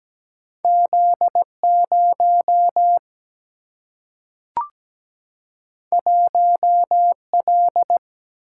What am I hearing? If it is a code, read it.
Morse code, heard as a keyed tone, about 17 wpm, 700 Hz. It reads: Z0 1L
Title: Single-Letter Single-Number 17wpm